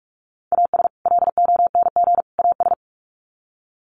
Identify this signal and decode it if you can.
Morse code, heard as a keyed tone, about 44 wpm, 700 Hz. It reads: AS LONG AS
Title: Sets of 3 Words 45wpm